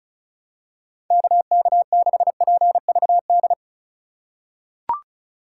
Morse code, heard as a keyed tone, about 34 wpm, 700 Hz. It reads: KK6PVD
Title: USA Call Signs - 6 Characters - 2x Character Spacing - 35wpm